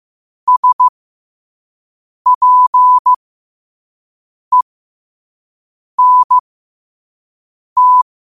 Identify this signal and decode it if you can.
Morse code, heard as a keyed tone, about 15 wpm, spaced as for 5 wpm, 1000 Hz. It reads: SPENT
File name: qotdfiles / 5wpm